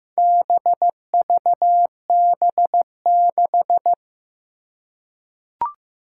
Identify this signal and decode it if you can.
Morse code, heard as a keyed tone, about 15 wpm, 700 Hz. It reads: BVB6
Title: Four Challenging Letter-Number Combinations 15wpm